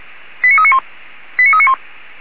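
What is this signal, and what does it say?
Five-tone selective call (ZVEI), heard as quick runs of five tones, tones of about 70 ms. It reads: 89391 89391
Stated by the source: ZVEI five-tone selective call 89386 received on an SDR.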